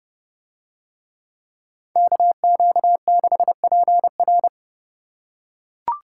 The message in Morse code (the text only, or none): KQ6PR